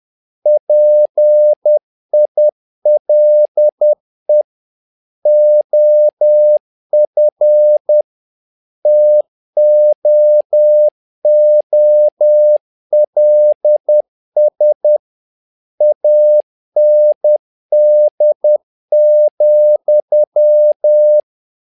Morse code, heard as a keyed tone, about 10 wpm, 600 Hz. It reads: PILE OF TOOLS AND,
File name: learning center / tower trouble / tower trouble 10wpm